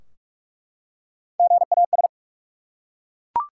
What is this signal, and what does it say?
Morse code, heard as a keyed tone, about 45 wpm, 700 Hz. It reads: GAS
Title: Three Letter Words - 45wpm